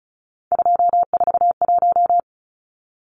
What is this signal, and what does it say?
Morse code, heard as a keyed tone, about 34 wpm, 700 Hz. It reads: 241
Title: Numbers - 3-Digit 35wpm